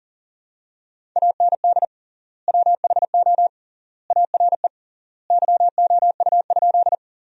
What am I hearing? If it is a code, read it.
Morse code, heard as a keyed tone, about 40 wpm, 700 Hz. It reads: AND WHO ARE YOU?
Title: 3x Word-Spacing - Sentences from Top 100 Words - 40wpm